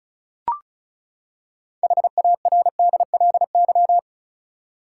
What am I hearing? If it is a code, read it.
Morse code, heard as a keyed tone, about 36 wpm, 700 Hz. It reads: HARDLY
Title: Six Letter Words - 35wpm